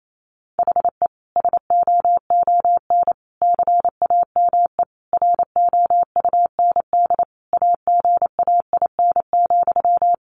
Morse code, heard as a keyed tone, about 28 wpm, 700 Hz. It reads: HE SOON CAME ROUND AGAIN,
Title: Sentences from Top 600 Words 28wpm